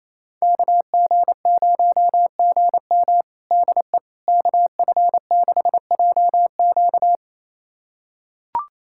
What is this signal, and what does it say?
Morse code, heard as a keyed tone, about 28 wpm, 700 Hz. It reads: KG0GM DE KF6JQ
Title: USA Callsign DE Callsign - 5 Characters - 28wpm